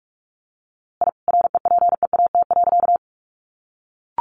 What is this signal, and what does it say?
Morse code, heard as a keyed tone, about 44 wpm, 700 Hz. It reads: I REPEAT.
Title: A Sound Of Thunder 45wpm